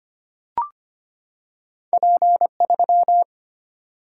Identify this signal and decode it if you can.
Morse code, heard as a keyed tone, about 25 wpm, 690 Hz. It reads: P3
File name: Two Letter-Number Combinations - 25wpm